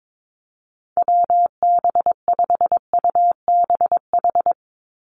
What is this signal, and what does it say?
Morse code, heard as a keyed tone, about 22 wpm, 700 Hz. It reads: WB5UBH
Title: USA Call Signs 22wpm